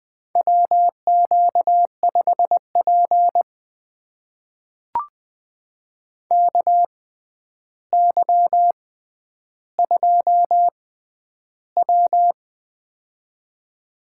Morse code, heard as a keyed tone, about 20 wpm, spaced as for 10 wpm, 700 Hz. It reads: WQ5P KY2W
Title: USA Call Signs - 6x Character Spacing - 20wpm